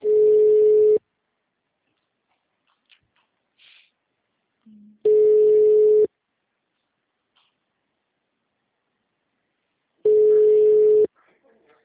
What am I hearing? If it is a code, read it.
Ringback tone (Europe, 425 Hz), with speech.
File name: out- -2033- - - .18864